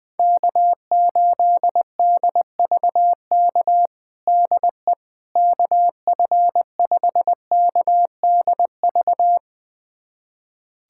Morse code, heard as a keyed tone, about 21 wpm, 700 Hz. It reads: K8DVK DE KF5KDV
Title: USA Callsign DE Callsign 20wpm